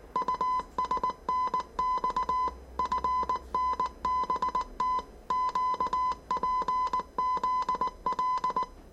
Morse code, heard as a keyed tone, about 19 wpm, 1010 Hz. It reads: USNX FNBT QPZL
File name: 业余无线电 / ●未分类 / 字码80